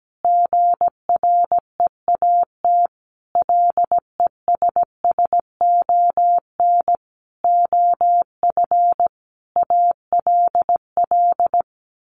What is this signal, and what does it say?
Morse code, heard as a keyed tone, about 17 wpm, 700 Hz. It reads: GREAT LESSON OF ALL
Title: Sentences from Top 1200 Words 17wpm